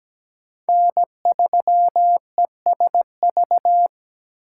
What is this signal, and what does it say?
Morse code, heard as a keyed tone, about 20 wpm, 700 Hz. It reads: N3ESV
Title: Speed Racing - USA 1x3 Call Signs 17wpm